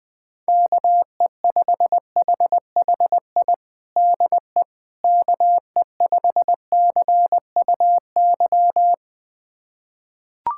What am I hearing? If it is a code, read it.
Morse code, heard as a keyed tone, about 20 wpm, 700 Hz. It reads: KE5HHI DE KE5CUY
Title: USA Callsign DE Callsign - 6 Characters - 20wpm